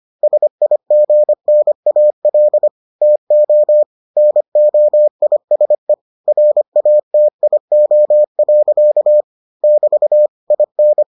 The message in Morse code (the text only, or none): SIGNAL TO NOISE RATIO. <BT> IN